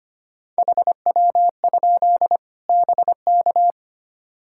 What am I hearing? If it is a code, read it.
Morse code, heard as a keyed tone, about 25 wpm, 700 Hz. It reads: HW? BK
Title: Practice QSO - Exchange 3 - More Info - 25wpm